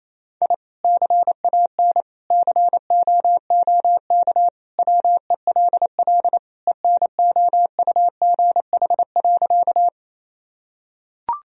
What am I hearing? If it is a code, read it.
Morse code, heard as a keyed tone, about 27 wpm, 700 Hz. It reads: I CAN COOK WELL ENOUGH.
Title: Sentences from Top 1000 Words - Review Part 4 - 28wpm